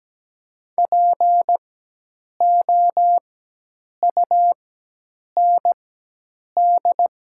Morse code, heard as a keyed tone, about 17 wpm, spaced as for 8 wpm, 700 Hz. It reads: POUND
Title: 4x Character Spacing - Top 500 Words - 17wpm